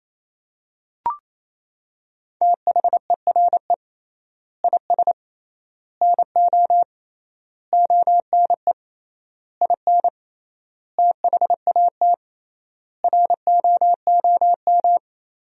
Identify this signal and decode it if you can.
Morse code, heard as a keyed tone, about 28 wpm, 700 Hz. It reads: THERE IS NO ONE IN THAT ROOM
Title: 3x Word-Spacing - Sentences from Top 500 Words - 28wpm